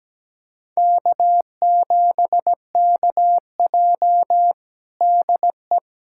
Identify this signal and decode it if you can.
Morse code, heard as a keyed tone, about 17 wpm, 700 Hz. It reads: K7KJ DE
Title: USA Callsign DE Callsign - 4 Characters - 17wpm